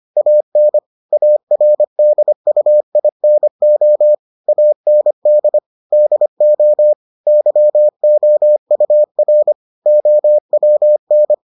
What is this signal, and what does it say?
Morse code, heard as a keyed tone, about 25 wpm, 600 Hz. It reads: AN ARDUINO AND DO YOUR OWN